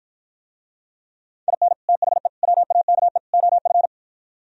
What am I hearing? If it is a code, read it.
Morse code, heard as a keyed tone, about 52 wpm, 700 Hz. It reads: IN THE WAKE OF